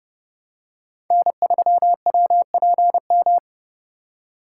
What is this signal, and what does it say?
Morse code, heard as a keyed tone, about 30 wpm, 700 Hz. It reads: N3WPM
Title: USA Call Signs - 4x Character Spacing - 30wpm